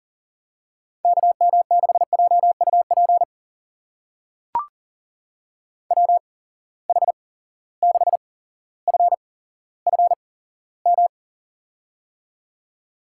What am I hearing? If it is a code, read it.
Morse code, heard as a keyed tone, about 40 wpm, spaced as for 17 wpm, 700 Hz. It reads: KM6JUP WH6FFM
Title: USA Call Signs - 6 Characters - 8x Character Spacing - 40wpm